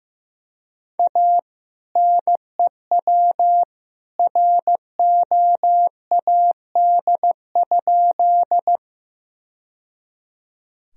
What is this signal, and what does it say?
Morse code, heard as a keyed tone, about 15 wpm, 700 Hz. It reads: A NEW ROAD?
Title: Sentences from Top 1000 Words - Review Part 2 - 15wpm